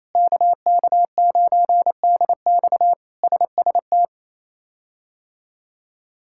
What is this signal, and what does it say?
Morse code, heard as a keyed tone, about 28 wpm, 700 Hz. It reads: KK9DX SST
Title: K1USN SST QSO Practice - 28wpm